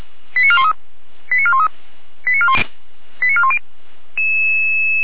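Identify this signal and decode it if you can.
Five-tone selective call (ZVEI), heard as quick runs of five tones, tones of about 70 ms. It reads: 89413 89413 89410
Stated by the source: ZVEI five-tone selective call received by SDR, from the Feuerwehr Orlamünde (fire brigade) audio archive.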